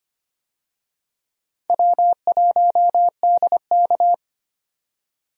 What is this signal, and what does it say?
Morse code, heard as a keyed tone, about 25 wpm, 700 Hz. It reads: W1DK
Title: USA Call Signs - 4 Characters - 25wpm